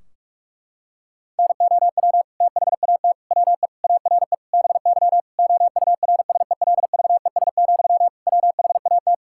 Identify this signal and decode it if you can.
Morse code, heard as a keyed tone, about 41 wpm, 700 Hz. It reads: NOW THAT WE ARE BY OURSELVES, WHAT
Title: Sentences from Top 1600 Words 45wpm